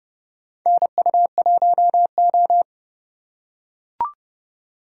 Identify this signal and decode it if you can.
Morse code, heard as a keyed tone, about 32 wpm, 700 Hz. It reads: NU1O
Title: K1USN SST QSO Practice - 30wpm